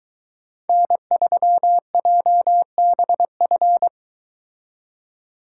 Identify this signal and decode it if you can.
Morse code, heard as a keyed tone, about 22 wpm, 700 Hz. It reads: N3JBF